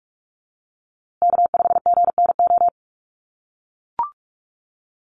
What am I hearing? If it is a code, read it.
Morse code, heard as a keyed tone, about 44 wpm, 700 Hz. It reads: K5GNO
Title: USA Call Signs Repeated 2x 45wpm